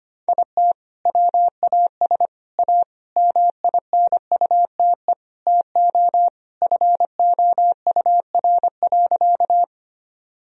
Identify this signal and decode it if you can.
Morse code, heard as a keyed tone, about 25 wpm, 700 Hz. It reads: IT WAS A MINUTE TO FOUR.
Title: Sentences from Top 1000 Words - Review Part 1 - 25wpm